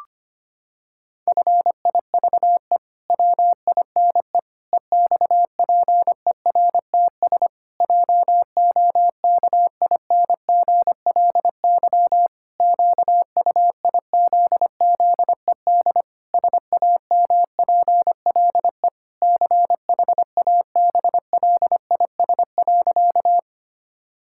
Morse code, heard as a keyed tone, about 25 wpm, 700 Hz. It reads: FIVE WINE EXPERTS JOKINGLY QUIZZED SAMPLE CHABLIS.
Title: Pangram Challenge v2 - 25wpm